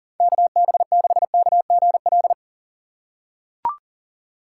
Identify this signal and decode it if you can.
Morse code, heard as a keyed tone, about 40 wpm, 700 Hz. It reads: KB6KGL
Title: USA Call Signs - 6 Characters - 4x Character Spacing - 40wpm